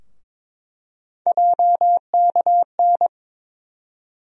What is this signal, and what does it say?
Morse code, heard as a keyed tone, about 22 wpm, 700 Hz. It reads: JKN
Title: USA Call Sign Postfixes - 3 Letters 22wpm